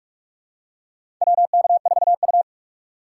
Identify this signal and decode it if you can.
Morse code, heard as a keyed tone, about 44 wpm, 690 Hz. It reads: WK4U
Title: USA Call Signs - 4 Characters - 3x Character Spacing - 45wpm